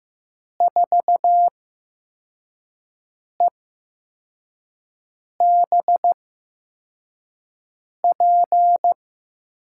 Morse code, heard as a keyed tone, about 15 wpm, spaced as for 4 wpm, 700 Hz. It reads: 4EBP
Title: USA Call Signs - 6 Characters - 8x Character Spacing - 15wpm